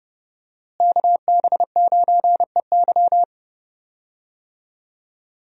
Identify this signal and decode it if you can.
Morse code, heard as a keyed tone, about 32 wpm, 700 Hz. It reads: KB9EY
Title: USA Call Signs - 5 Characters - Repeated 2x - 30wpm